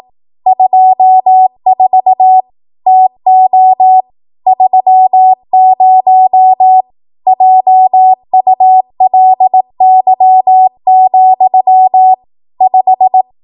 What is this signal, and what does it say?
Morse code, heard as a keyed tone, about 18 wpm, 750 Hz. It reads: 24 TO 30 JULY, 5
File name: Morse / 18WPM